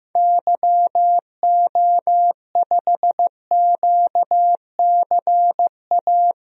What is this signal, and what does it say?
Morse code, heard as a keyed tone, about 15 wpm, 700 Hz. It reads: YO5QCA
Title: DXCC-100 Call Signs - 6 Characters - 15wpm